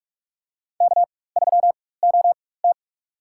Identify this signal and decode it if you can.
Morse code, heard as a keyed tone, about 42 wpm, spaced as for 21 wpm, 700 Hz. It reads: K3OT
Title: USA Call Signs - 4 Characters - 4x Character Spacing - 45wpm